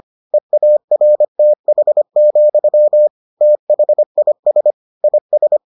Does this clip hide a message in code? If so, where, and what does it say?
Yes: Morse code from 0.3 to 5.6 s, heard as a keyed tone, about 25 wpm, 600 Hz. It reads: EARTH, THIS IS